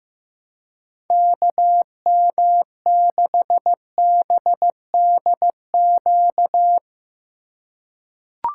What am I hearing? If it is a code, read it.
Morse code, heard as a keyed tone, about 16 wpm, 700 Hz. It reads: KM6BDQ
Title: USA Call Signs - 4x Character Spacing - 15wpm